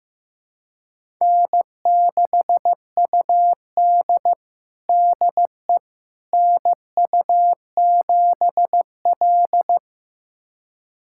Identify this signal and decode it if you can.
Morse code, heard as a keyed tone, about 15 wpm, 700 Hz. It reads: N6UD DE NU7L
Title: USA Callsign DE Callsign - 4 Characters - 15wpm